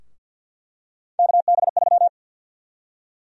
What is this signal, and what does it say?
Morse code, heard as a keyed tone, about 51 wpm, 700 Hz. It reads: KB3
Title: USA Call Sign Prefixes - Repeated 2x - 50wpm